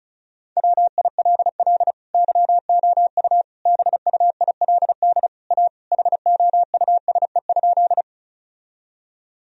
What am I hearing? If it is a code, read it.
Morse code, heard as a keyed tone, about 34 wpm, 700 Hz. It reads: WILL YOU BUILD A HOUSE?
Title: Sentences from Top 1000 Words - Review Part 1 - 35wpm